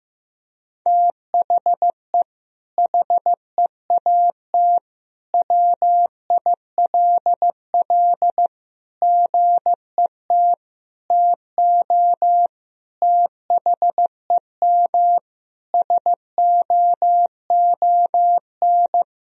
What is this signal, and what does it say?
Morse code, heard as a keyed tone, about 15 wpm, 700 Hz. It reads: THE HEAT WILL GET TO THEM SOON
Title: Sentences from Top 1000 Words - Review Part 3 - 15wpm